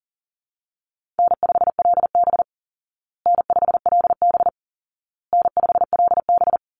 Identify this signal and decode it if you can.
Morse code, heard as a keyed tone, about 40 wpm, 700 Hz. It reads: N5LB N5LB N5LB
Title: Easy Call Signs Repeated 3x - 40wpm